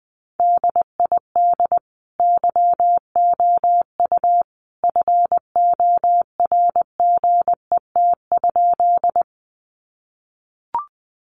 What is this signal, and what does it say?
Morse code, heard as a keyed tone, about 20 wpm, 700 Hz. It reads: DID YOU FORGET?
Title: Sentences from Top 1200 Words 20wpm